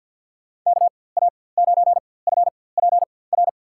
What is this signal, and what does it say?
Morse code, heard as a keyed tone, about 52 wpm, spaced as for 24 wpm, 700 Hz. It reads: KA9FPR
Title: USA Call Signs - 6 Characters - 4x Character Spacing - 50wpm